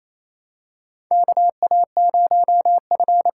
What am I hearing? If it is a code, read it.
Morse code, heard as a keyed tone, about 28 wpm, 700 Hz. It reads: KA0F
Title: USA Call Signs - 6 Characters - 6x Character Spacing - 28wpm